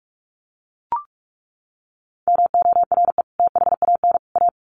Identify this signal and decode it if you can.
Morse code, heard as a keyed tone, about 44 wpm, 700 Hz. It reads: MORE THAN A
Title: Sets of 3 Words 45wpm